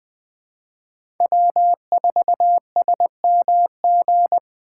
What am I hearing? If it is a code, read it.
Morse code, heard as a keyed tone, about 20 wpm, 700 Hz. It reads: W4SMG
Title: USA Call Signs - 5 Characters - 8x Character Spacing - 20wpm